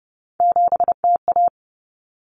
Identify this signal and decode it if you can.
Morse code, heard as a keyed tone, about 30 wpm, 700 Hz. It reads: 7TA
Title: Common Cut Numbers - 3-Digit 30wpm